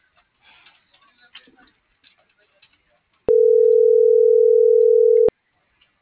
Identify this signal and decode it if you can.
Ringback tone (North America), 440 + 480 Hz.